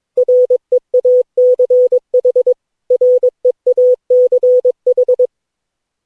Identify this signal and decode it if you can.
Morse code, heard as a keyed tone, about 22 wpm, 500 Hz. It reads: REACH REACH
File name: cw / Session9 cw training